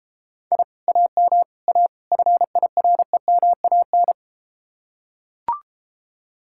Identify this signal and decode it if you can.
Morse code, heard as a keyed tone, about 33 wpm, 700 Hz. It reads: I AM A FIREMAN